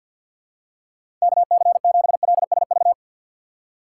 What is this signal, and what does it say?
Morse code, heard as a keyed tone, about 49 wpm, 700 Hz. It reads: KC7LSV
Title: USA Call Signs - 6 Characters - 6x Character Spacing - 50wpm